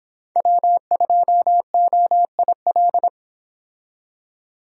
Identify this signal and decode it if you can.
Morse code, heard as a keyed tone, about 26 wpm, 700 Hz. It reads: W2OIL